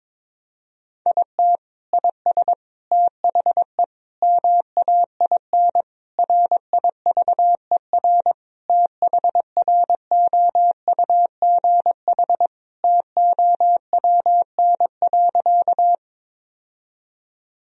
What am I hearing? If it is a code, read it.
Morse code, heard as a keyed tone, about 22 wpm, 700 Hz. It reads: IT IS THE MAIN RIVER THROUGH TOWN.
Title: Sentences from Top 1000 Words - Review Part 2 - 22wpm